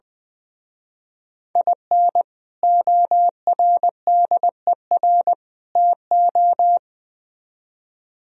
Morse code, heard as a keyed tone, about 24 wpm, 700 Hz. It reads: IN ORDER TO
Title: Speed-Racing - Sets of 3 Words 20wpm